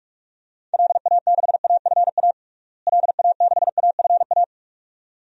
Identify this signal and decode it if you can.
Morse code, heard as a keyed tone, about 46 wpm, 700 Hz. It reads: LA6AFA LA6AFA
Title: DXCC-100 Call Signs - 6 Characters - Repeated 2x - 45wpm